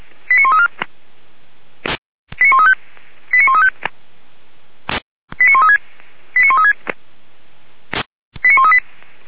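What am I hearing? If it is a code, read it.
Five-tone selective call (ZVEI), heard as quick runs of five tones, tones of about 70 ms. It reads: 89134 89135 89135 89136 89136 89138
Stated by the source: ZVEI five-tone call received by SDR, fire brigade Orlamünde (feuerwehr-orlamuende) recording.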